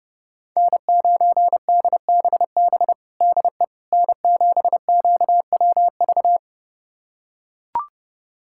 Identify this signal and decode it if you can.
Morse code, heard as a keyed tone, about 32 wpm, 700 Hz. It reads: N9DBB DE N7QWV
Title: USA Callsign DE Callsign - 5 Characters - 30wpm